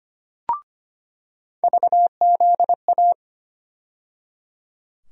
Morse code, heard as a keyed tone, about 25 wpm, 700 Hz. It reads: VZA
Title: USA Call Sign Postfixes - 3 Letters 25wpm